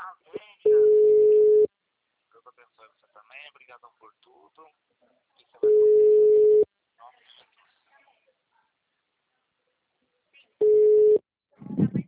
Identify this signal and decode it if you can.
Ringback tone (Europe, 425 Hz), with speech.
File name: out- -2007- - - .69514